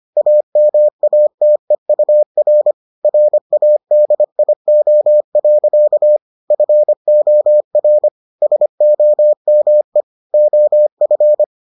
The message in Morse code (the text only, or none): AMATEUR RADIO. FOR SOME OF